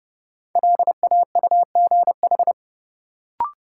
Morse code, heard as a keyed tone, about 30 wpm, 700 Hz. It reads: LAUGH
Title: Sentences from Top 500 Words - New Word Drill - 30wpm